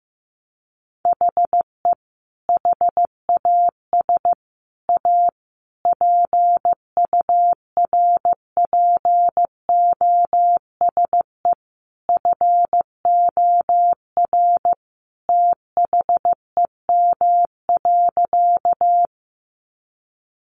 Morse code, heard as a keyed tone, about 15 wpm, 700 Hz. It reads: HE HAS A PURPOSE FOR THEM.
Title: Sentences from Top 800 Words 15wpm